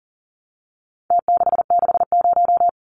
Morse code, heard as a keyed tone, about 40 wpm, 700 Hz. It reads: T660
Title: Numbers - Contest Style - 4-Digit 40wpm